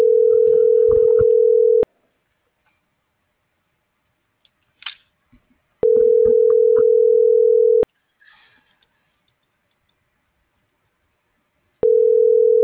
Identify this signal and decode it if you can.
Ringback tone (North America), 440 + 480 Hz.